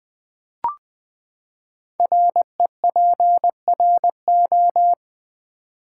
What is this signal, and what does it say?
Morse code, heard as a keyed tone, about 20 wpm, 700 Hz. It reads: REPRO
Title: Build A Word - 5 Letters At A Time - 12-Letter Words - 20wpm